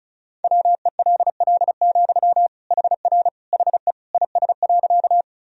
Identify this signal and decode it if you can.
Morse code, heard as a keyed tone, about 34 wpm, 700 Hz. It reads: WELL, HR HE IS.
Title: Sentences from Top 200 Words and Abbreviations 35wpm